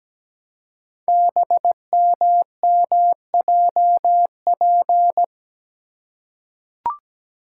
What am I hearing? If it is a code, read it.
Morse code, heard as a keyed tone, about 17 wpm, 700 Hz. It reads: BMMJP
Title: Code Groups - 5 Letter:Numbers - 17wpm